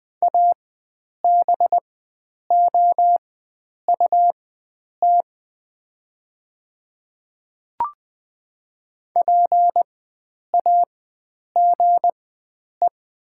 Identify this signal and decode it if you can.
Morse code, heard as a keyed tone, about 20 wpm, spaced as for 9 wpm, 700 Hz. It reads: ABOUT PAGE
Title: 4x Character Spacing - Top 300 Words - 20wpm